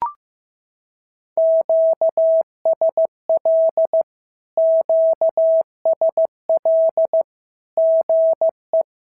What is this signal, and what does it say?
Morse code, heard as a keyed tone, about 15 wpm, 650 Hz. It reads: QSL QSL GE
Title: SOTA QSO Practice 15wpm